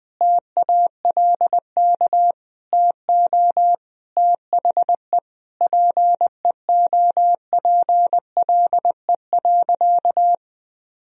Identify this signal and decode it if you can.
Morse code, heard as a keyed tone, about 20 wpm, 700 Hz. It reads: TALK TO THE PEOPLE.
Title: Sentences from Top 400 Words and Abbreviations 20wpm